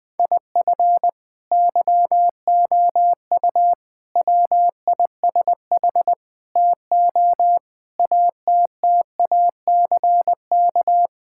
Morse code, heard as a keyed tone, about 21 wpm, 700 Hz. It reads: IF YOU WISH TO ATTACK
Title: Sentences from Top 1600 Words 20wpm